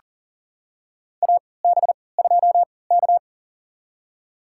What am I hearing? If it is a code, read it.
Morse code, heard as a keyed tone, about 40 wpm, spaced as for 23 wpm, 700 Hz. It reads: AB2K
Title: USA Call Signs - 4 Characters - 3x Character Spacing - 40wpm